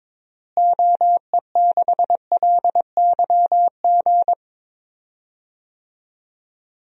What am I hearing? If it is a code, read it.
Morse code, heard as a keyed tone, about 22 wpm, 700 Hz. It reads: OE6LYG
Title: DXCC-100 Call Signs - 6 Characters - Repeated 2x - 22wpm